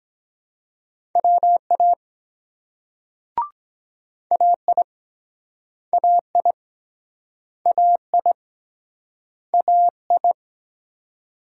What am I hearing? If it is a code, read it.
Morse code, heard as a keyed tone, about 24 wpm, 700 Hz. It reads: WA AI AI AI AI